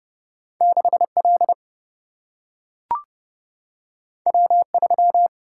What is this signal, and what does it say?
Morse code, heard as a keyed tone, about 30 wpm, 700 Hz. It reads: 6 L W 3
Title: Single-Letter Single-Number 30wpm